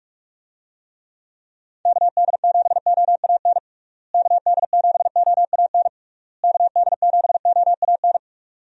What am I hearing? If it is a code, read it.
Morse code, heard as a keyed tone, about 41 wpm, 700 Hz. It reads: KD7OAN KD7OAN KD7OAN
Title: USA Call Signs - 6 Characters - Repeated 3x - 45wpm